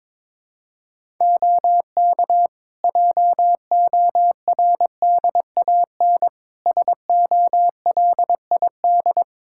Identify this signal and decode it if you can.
Morse code, heard as a keyed tone, about 22 wpm, 700 Hz. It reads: OK JORDAN SOLID
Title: Practice QSO - Exchange 3 - More Info - 22wpm